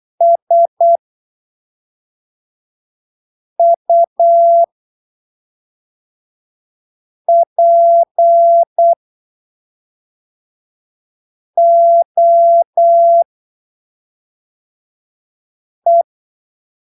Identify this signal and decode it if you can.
Morse code, heard as a keyed tone, about 8 wpm, spaced as for 3 wpm, 680 Hz. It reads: SUPOE
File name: arquivos / texto / texto 26 08ppm